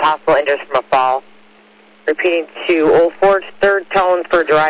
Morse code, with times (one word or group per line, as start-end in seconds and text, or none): none